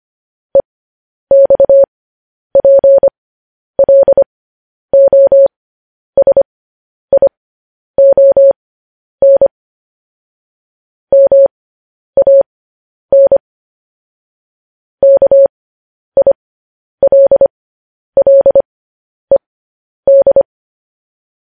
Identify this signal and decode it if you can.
Morse code, heard as a keyed tone, about 25 wpm, spaced as for 10 wpm, 550 Hz. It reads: EXPLOSION MAN KILLED
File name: podcast / News headlines-1x-25x10